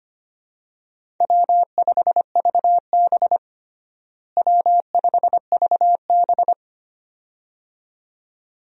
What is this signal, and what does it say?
Morse code, heard as a keyed tone, about 25 wpm, 700 Hz. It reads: W5VB W5VB
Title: USA Call Signs - 4 Characters - Repeated 2x - 25wpm